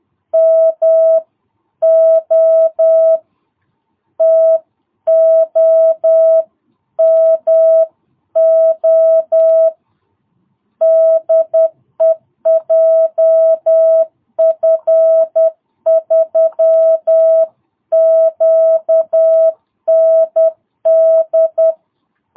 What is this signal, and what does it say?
Morse code, heard as a keyed tone, about 10 wpm, 660 Hz. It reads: MO TOMO DE JF3QND
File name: jh8cbh / 2022cw